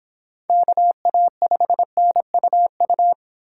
Morse code, heard as a keyed tone, about 25 wpm, 700 Hz. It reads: KA5NUU